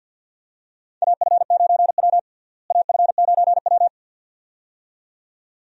Morse code, heard as a keyed tone, about 52 wpm, 700 Hz. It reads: AF9W AF9W
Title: USA Call Signs - 4 Characters - Repeated 2x - 50wpm